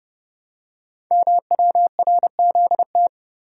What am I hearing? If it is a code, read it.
Morse code, heard as a keyed tone, about 30 wpm, 700 Hz. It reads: MWRZT
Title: Code Groups - 5 Letters - 30wpm